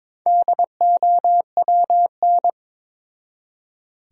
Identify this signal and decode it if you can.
Morse code, heard as a keyed tone, about 21 wpm, 700 Hz. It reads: DOWN
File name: Build A Phrase - 2-Word Phrases - 22wpm